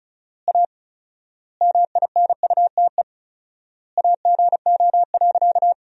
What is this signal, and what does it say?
Morse code, heard as a keyed tone, about 36 wpm, 700 Hz. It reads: A MINUTE AGO.
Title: 4x Word-Spacing - Sentences from Top 500 Words - 35wpm